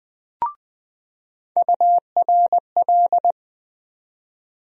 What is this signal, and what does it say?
Morse code, heard as a keyed tone, about 20 wpm, 700 Hz. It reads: URL
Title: USA Call Sign Postfixes - 3 Letters 20wpm